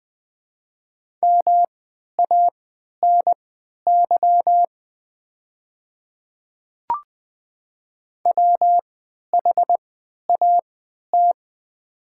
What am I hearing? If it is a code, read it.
Morse code, heard as a keyed tone, about 20 wpm, spaced as for 11 wpm, 700 Hz. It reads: MANY WHAT
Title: 3x Character Spacing - Top 100 Words - 20wpm